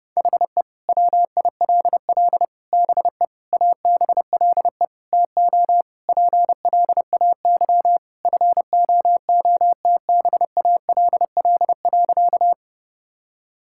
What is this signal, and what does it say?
Morse code, heard as a keyed tone, about 32 wpm, 700 Hz. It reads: HE WILL BE ABLE TO PLAY FOOTBALL.
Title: Sentences from Top 1600 Words 30wpm